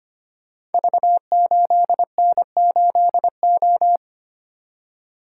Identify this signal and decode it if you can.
Morse code, heard as a keyed tone, about 25 wpm, 700 Hz. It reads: V8N8O
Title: Code Groups - 5 Letter:Numbers - 25wpm